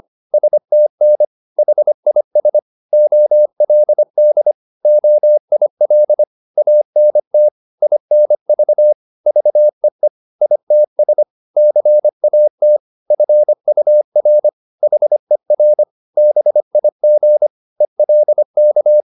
Morse code, heard as a keyed tone, about 25 wpm, 600 Hz. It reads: STN HIS OLD OIL ANT INV VEE ITS CAT FUR HER BIG ELK